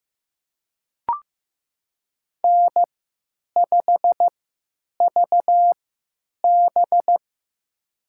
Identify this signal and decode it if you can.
Morse code, heard as a keyed tone, about 15 wpm, spaced as for 9 wpm, 700 Hz. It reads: N5VB
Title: USA Call Signs - 4 Characters - 3x Character Spacing - 15wpm